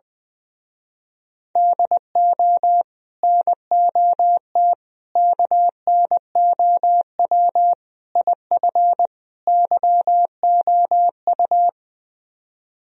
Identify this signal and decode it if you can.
Morse code, heard as a keyed tone, about 20 wpm, 700 Hz. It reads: DO NOT KNOW IF YOU
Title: Speed-Racing - Sets of 5 Words 20wpm